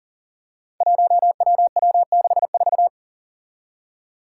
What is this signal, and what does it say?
Morse code, heard as a keyed tone, about 44 wpm, 700 Hz. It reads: 1WW64
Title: Five Challenging Letter-Number Combinations 40wpm